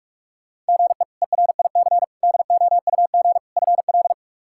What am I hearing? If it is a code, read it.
Morse code, heard as a keyed tone, about 43 wpm, 700 Hz. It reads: GE ERIC DOUG FL
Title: K1USN SST QSO Practice - 45wpm